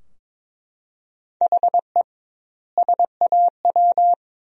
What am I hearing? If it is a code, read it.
Morse code, heard as a keyed tone, about 22 wpm, 700 Hz. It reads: HE SAW
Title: 2x Word-Spacing - Sentences from Top 500 Words - 22wpm